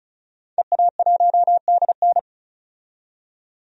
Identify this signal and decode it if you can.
Morse code, heard as a keyed tone, about 36 wpm, 700 Hz. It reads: EA1DN
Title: DXCC-100 Call Signs - 5 Characters - 35wpm